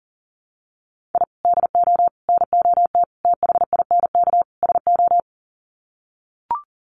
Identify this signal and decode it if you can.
Morse code, heard as a keyed tone, about 40 wpm, 700 Hz. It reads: I DO NOT THINK SO
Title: Sets of 5 Words 40wpm